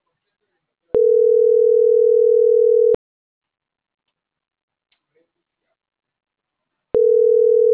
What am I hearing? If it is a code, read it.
Ringback tone (North America), 440 + 480 Hz.
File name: CENTRAL MAERSK / out gabrielle.melendez